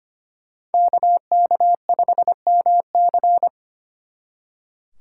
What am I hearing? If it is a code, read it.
Morse code, heard as a keyed tone, about 32 wpm, 700 Hz. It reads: KK5MC